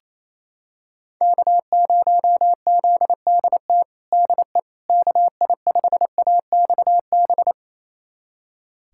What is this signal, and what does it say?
Morse code, heard as a keyed tone, about 28 wpm, 700 Hz. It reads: K0ZDT DE KI5AXB
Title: USA Callsign DE Callsign 28wpm